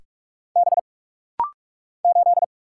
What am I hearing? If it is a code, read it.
Morse code, heard as a keyed tone, about 44 wpm, 700 Hz. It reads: B 8
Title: Single Letter-Number Alt - Rapid-Fire 45wpm